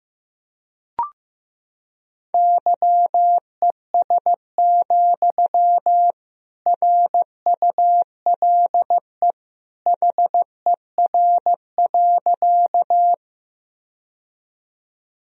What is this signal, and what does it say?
Morse code, heard as a keyed tone, about 15 wpm, 700 Hz. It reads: YES, RULE HER.
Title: Sentences from Top 1000 Words - Review Part 2 - 15wpm